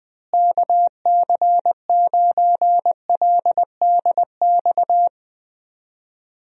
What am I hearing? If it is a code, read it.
Morse code, heard as a keyed tone, about 20 wpm, 700 Hz. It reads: KC9LDX
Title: USA Call Signs - 6 Characters - Repeated 3x - 20wpm